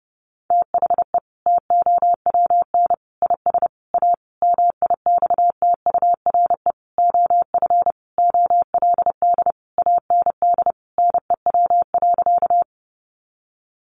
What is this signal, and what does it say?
Morse code, heard as a keyed tone, about 30 wpm, 700 Hz. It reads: THE TOWN IS A MIXTURE OF OLD AND NEW.
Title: Sentences from Top 1500 Words 30wpm